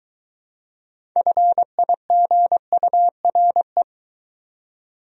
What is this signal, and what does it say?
Morse code, heard as a keyed tone, about 22 wpm, 700 Hz. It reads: FIGURE